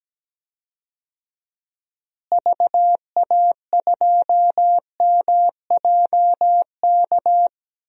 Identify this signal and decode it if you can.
Morse code, heard as a keyed tone, about 17 wpm, 700 Hz. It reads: VA2MJK
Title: DXCC-100 Call Signs - 6 Characters - Repeated 3x - 17wpm